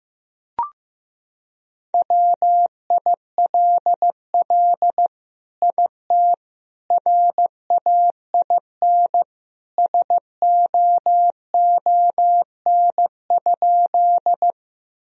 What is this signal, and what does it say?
Morse code, heard as a keyed tone, about 15 wpm, 700 Hz. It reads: WILL IT RAIN SOON?
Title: Sentences from Top 1000 Words - Review Part 1 - 15wpm